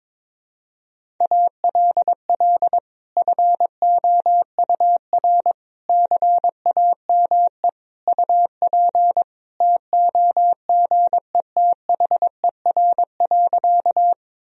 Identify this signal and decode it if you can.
Morse code, heard as a keyed tone, about 21 wpm, 700 Hz. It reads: ALL FOUR CAME UP TOGETHER.
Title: Sentences from Top 1000 Words - Review Part 1 - 22wpm